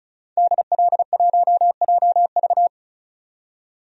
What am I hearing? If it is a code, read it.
Morse code, heard as a keyed tone, about 34 wpm, 700 Hz. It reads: DL1JV
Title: DXCC-100 Call Signs - 5 Characters - 35wpm